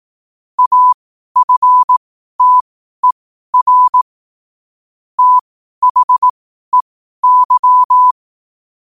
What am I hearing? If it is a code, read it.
Morse code, heard as a keyed tone, about 18 wpm, spaced as for 12 wpm, 1000 Hz. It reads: AFTER THEY
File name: qotdfiles / 12wpm